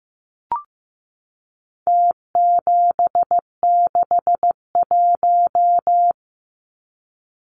Morse code, heard as a keyed tone, about 15 wpm, 700 Hz. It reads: T761
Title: Numbers - Contest Style - 4-Digit 15wpm